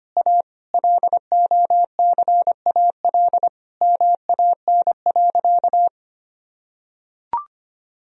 Morse code, heard as a keyed tone, about 25 wpm, 700 Hz. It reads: A LOCAL MAN.
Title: Sentences from Top 1000 Words - Review Part 3 - 25wpm